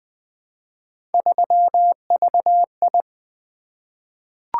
Morse code, heard as a keyed tone, about 20 wpm, 700 Hz. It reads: 3VI
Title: Three Letter-Number Combinations - 20wpm